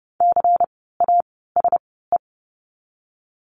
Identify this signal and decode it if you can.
Morse code, heard as a keyed tone, about 30 wpm, spaced as for 17 wpm, 700 Hz. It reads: CASE
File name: Lesson 16f - Letter c - Words 30wpm